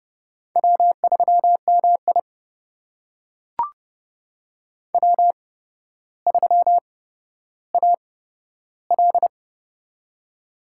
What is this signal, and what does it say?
Morse code, heard as a keyed tone, about 30 wpm, spaced as for 13 wpm, 700 Hz. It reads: W3MI W3AL
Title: USA Call Signs - 4 Characters - 8x Character Spacing - 30wpm